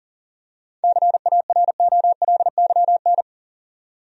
Morse code, heard as a keyed tone, about 40 wpm, 700 Hz. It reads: CAROLYN
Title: USA Common Names - 40wpm